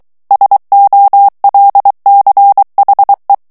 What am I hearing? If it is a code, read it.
Morse code, heard as a keyed tone, about 23 wpm, 800 Hz. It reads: SOLCHE
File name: morse mp3 / ALLEMAND texte clair 1 vitesse 22 mots minute